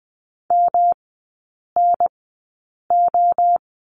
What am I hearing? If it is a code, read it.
Morse code, heard as a keyed tone, about 20 wpm, spaced as for 8 wpm, 700 Hz. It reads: MNO
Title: 114 Lesson 19e - Letter w - Three Characters 20wpm